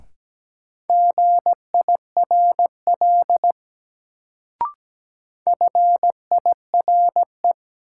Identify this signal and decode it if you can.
Morse code, heard as a keyed tone, about 17 wpm, 700 Hz. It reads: GIRL FIRE
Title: Sentences from Top 400 Words - New Word Drill - 17wpm